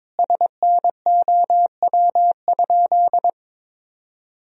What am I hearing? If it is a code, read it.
Morse code, heard as a keyed tone, about 22 wpm, 700 Hz. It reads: SNOW?
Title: Sentences from Top 1000 Words - Review Part 3 - 22wpm